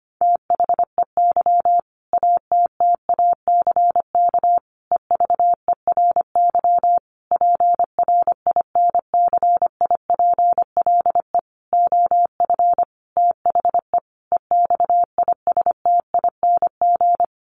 Morse code, heard as a keyed tone, about 25 wpm, 700 Hz. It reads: THEY ATTACK EVERY PRINCIPLE OF THE EXISTING
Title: Sentences from Top 1500 Words 25wpm